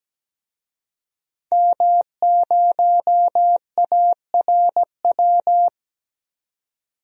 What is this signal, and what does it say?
Morse code, heard as a keyed tone, about 17 wpm, 700 Hz. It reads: M0ARW
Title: DXCC-100 Call Signs - 5 Characters - 17wpm